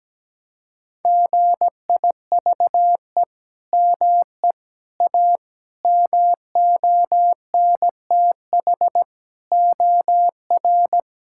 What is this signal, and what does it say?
Morse code, heard as a keyed tone, about 17 wpm, 700 Hz. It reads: GIVE ME A MONTH OR
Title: Sentences from Top 1000 Words - Review Part 5 - 17wpm